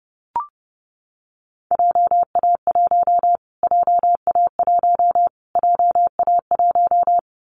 Morse code, heard as a keyed tone, about 30 wpm, 700 Hz. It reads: JA1 JA1 JA1
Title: Callsign Prefixes for Top 25 DXCC - Repeated 3x - 30wpm